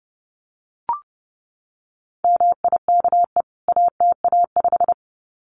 Morse code, heard as a keyed tone, about 30 wpm, 700 Hz. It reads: MIKE ATA5
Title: CWT Exchange 30wpm